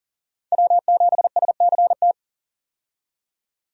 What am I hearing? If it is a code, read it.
Morse code, heard as a keyed tone, about 40 wpm, 700 Hz. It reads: W7SCT
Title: USA Call Signs - 5 Characters - 40wpm